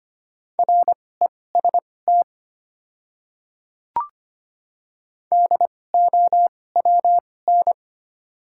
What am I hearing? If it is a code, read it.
Morse code, heard as a keyed tone, about 25 wpm, spaced as for 18 wpm, 700 Hz. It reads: REST DOWN
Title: 2x Character Spacing - Top 500 Words - 25wpm